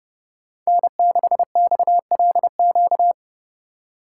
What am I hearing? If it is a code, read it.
Morse code, heard as a keyed tone, about 29 wpm, 700 Hz. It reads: N6XLQ